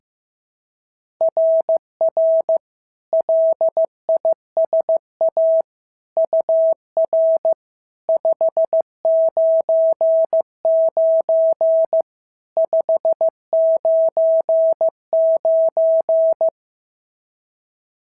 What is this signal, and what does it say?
Morse code, heard as a keyed tone, about 15 wpm, 650 Hz. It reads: RR LISA UR 599 599
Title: SOTA QSO Practice 15wpm